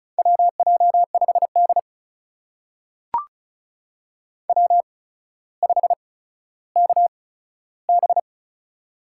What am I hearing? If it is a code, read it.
Morse code, heard as a keyed tone, about 36 wpm, spaced as for 15 wpm, 700 Hz. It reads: WJ5D W5KB
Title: USA Call Signs - 8x Character Spacing - 35wpm